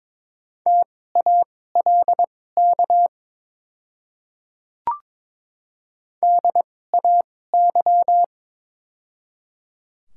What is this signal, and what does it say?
Morse code, heard as a keyed tone, about 22 wpm, spaced as for 16 wpm, 700 Hz. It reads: TALK DAY
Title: 2x Character Spacing - Top 500 Words - 22wpm